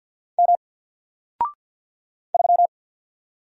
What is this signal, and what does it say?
Morse code, heard as a keyed tone, about 49 wpm, 700 Hz. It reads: M 3
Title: Single Character (Letter:Number:Punctuation:Prosign) Alt 50wpm